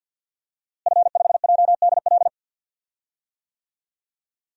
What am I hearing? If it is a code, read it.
Morse code, heard as a keyed tone, about 50 wpm, 700 Hz. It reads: F5JDL
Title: DXCC-100 Call Signs - 5 Characters - Repeated 2x - 50wpm